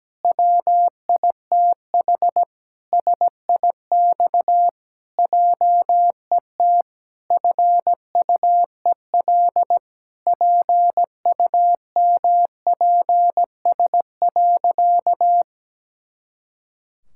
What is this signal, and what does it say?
Morse code, heard as a keyed tone, about 17 wpm, 700 Hz. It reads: WITH SIX JET FUEL PUMPS.
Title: Pangram Challenge v2 - 17wpm